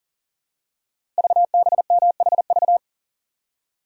Morse code, heard as a keyed tone, about 42 wpm, 690 Hz. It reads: VBMHV
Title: Code Groups - 5 Letters - 40wpm